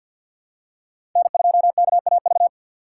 Morse code, heard as a keyed tone, about 52 wpm, 700 Hz. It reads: N2KAV
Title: USA Call Signs - 5 Characters - 3x Character Spacing - 50wpm